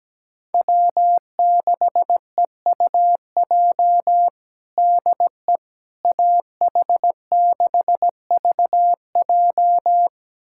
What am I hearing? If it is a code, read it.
Morse code, heard as a keyed tone, about 17 wpm, 700 Hz. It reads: W6EUJ DE AH6VJ
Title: USA Callsign DE Callsign - 5 Characters - 17wpm